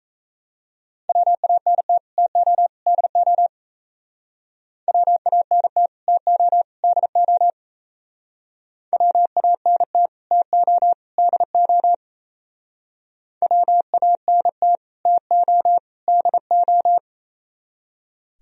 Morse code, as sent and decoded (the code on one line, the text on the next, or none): .-- .- -. - / - --- / -.. --- / .-- .- -. - / - --- / -.. --- / .-- .- -. - / - --- / -.. --- / .-- .- -. - / - --- / -.. ---
WANT TO DO WANT TO DO WANT TO DO WANT TO DO